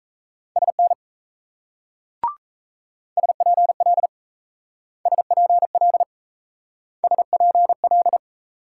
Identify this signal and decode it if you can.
Morse code, heard as a keyed tone, about 34 wpm, 700 Hz. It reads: SN SPL SPL SPL
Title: Speed-Racing Multi-letter Phonemes - 28wpm